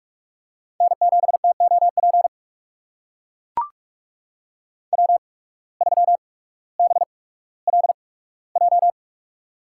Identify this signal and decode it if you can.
Morse code, heard as a keyed tone, about 44 wpm, spaced as for 19 wpm, 700 Hz. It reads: N7TOP W3BLJ
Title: USA Call Signs - 5 Characters - 8x Character Spacing - 45wpm